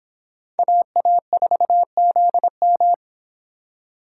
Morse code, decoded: AA4ZM

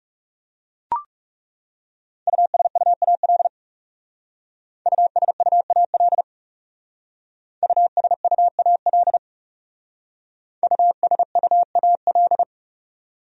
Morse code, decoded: USUAL USUAL USUAL USUAL

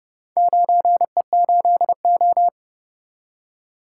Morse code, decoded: 9E8O

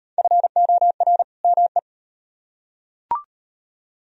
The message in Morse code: FOR ME

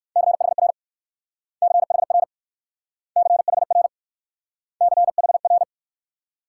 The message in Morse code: CHR CHR CHR CHR

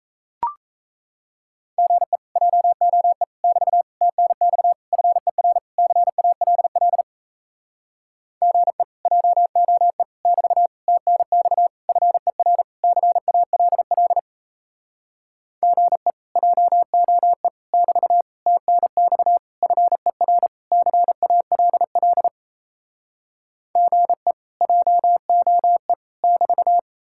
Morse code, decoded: GE JOE <BT> TNX FER CALL GE JOE <BT> TNX FER CALL GE JOE <BT> TNX FER CALL GE JOE <BT>

